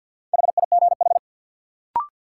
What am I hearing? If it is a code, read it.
Morse code, heard as a keyed tone, about 51 wpm, 700 Hz. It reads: HIGH
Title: Sets of 2 Words - New Word Drill - 50wpm